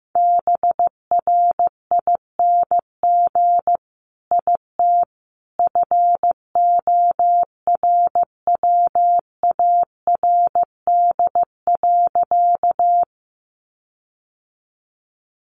Morse code, heard as a keyed tone, about 15 wpm, 700 Hz. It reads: BRING IT FORWARD.
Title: Sentences from Top 1300 Words 15wpm